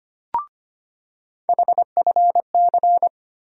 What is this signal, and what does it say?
Morse code, heard as a keyed tone, about 25 wpm, 700 Hz. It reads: HFC
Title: USA Call Sign Postfixes - 3 Letters 25wpm